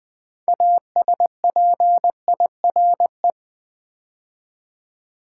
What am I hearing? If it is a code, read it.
Morse code, heard as a keyed tone, about 20 wpm, 700 Hz. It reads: ASPIRE
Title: Build A Word - 3 Letters At A Time - 6-Letter Words - 20wpm